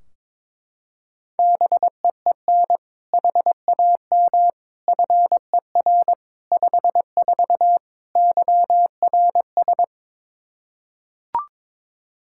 Morse code, heard as a keyed tone, about 22 wpm, 700 Hz. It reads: BEEN HAM FER 54 YRS
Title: K1USN SST QSO Practice 22wpm